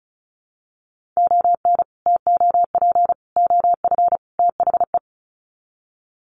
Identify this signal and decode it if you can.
Morse code, heard as a keyed tone, about 34 wpm, 700 Hz. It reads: ON TOP OF THE
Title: Sets of 4 Words 35wpm